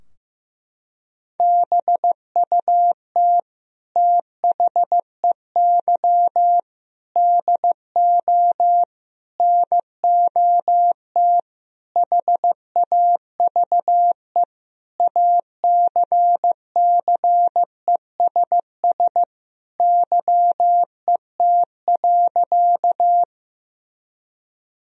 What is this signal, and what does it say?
Morse code, heard as a keyed tone, about 15 wpm, 700 Hz. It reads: BUT THEY DO NOT HAVE ACCESS YET.
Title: Sentences from Top 1000 Words - Review Part 4 - 15wpm